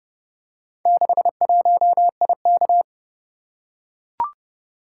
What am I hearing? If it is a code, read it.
Morse code, heard as a keyed tone, about 29 wpm, 700 Hz. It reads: 61IK